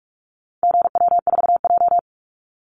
Morse code, heard as a keyed tone, about 44 wpm, 700 Hz. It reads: GW4J
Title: Call Signs Repeated 2x - 45wpm